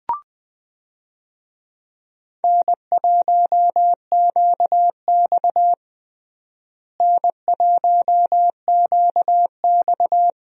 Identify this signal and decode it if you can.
Morse code, heard as a keyed tone, about 20 wpm, 700 Hz. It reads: N1QX N1QX
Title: USA Call Signs - 4 Characters - Repeated 3x - 20wpm